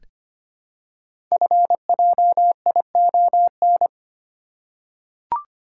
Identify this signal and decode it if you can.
Morse code, heard as a keyed tone, about 25 wpm, 700 Hz. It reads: FJION
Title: Code Groups - 5 Letters - 25wpm